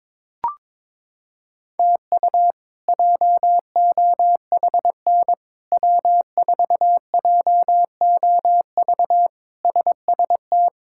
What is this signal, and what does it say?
Morse code, heard as a keyed tone, about 22 wpm, 700 Hz. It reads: TU JOHN W4JOV SST
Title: K1USN SST QSO Practice - 22wpm